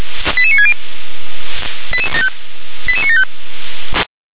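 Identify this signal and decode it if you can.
Five-tone selective call (ZVEI), heard as quick runs of five tones, tones of about 70 ms. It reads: 88960 88975
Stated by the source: ZVEI five-tone call received by SDR.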